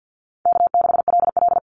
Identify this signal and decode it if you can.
Morse code, heard as a keyed tone, about 48 wpm, 700 Hz. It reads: K6LL
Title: Call Signs Repeated 3x - 50wpm